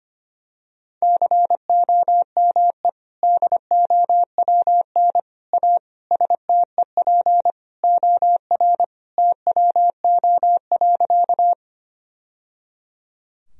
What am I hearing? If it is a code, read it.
Morse code, heard as a keyed tone, about 25 wpm, 700 Hz. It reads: COME DOWN A STEP OR TWO.
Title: Sentences from Top 1000 Words - Review Part 2 - 25wpm